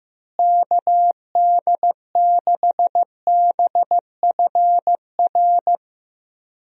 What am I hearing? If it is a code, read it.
Morse code, heard as a keyed tone, about 16 wpm, 700 Hz. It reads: KD6BFR
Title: USA Call Signs - 6 Characters - Repeated 2x - 15wpm